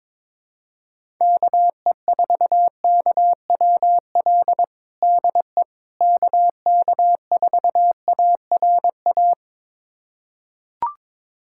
Morse code, heard as a keyed tone, about 22 wpm, 700 Hz. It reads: KE4KWL DE KK4ARA
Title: USA Callsign DE Callsign - 6 Characters - 22wpm